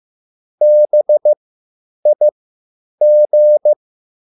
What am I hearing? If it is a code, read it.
Morse code, heard as a keyed tone, about 15 wpm, spaced as for 8 wpm, 600 Hz. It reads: BIG